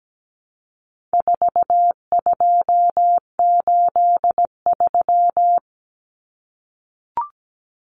Morse code, heard as a keyed tone, about 17 wpm, 700 Hz. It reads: 4283
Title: Common Cut Numbers - 4-Digit 17wpm